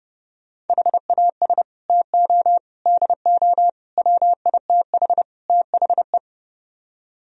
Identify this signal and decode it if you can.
Morse code, heard as a keyed tone, about 38 wpm, 700 Hz. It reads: HAS TO DO WITH THE